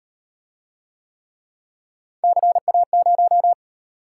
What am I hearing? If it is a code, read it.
Morse code, heard as a keyed tone, about 38 wpm, 700 Hz. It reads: CA0